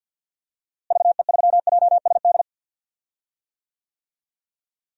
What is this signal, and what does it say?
Morse code, heard as a keyed tone, about 45 wpm, 700 Hz. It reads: VE3JSD
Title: DXCC-100 Call Signs - 6 Characters - Repeated 3x - 50wpm